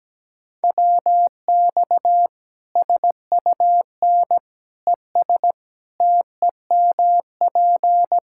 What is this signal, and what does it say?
Morse code, heard as a keyed tone, about 17 wpm, 700 Hz. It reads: WX SUN ES TEMP
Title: Practice QSO - Exchange 2 - Info - 17wpm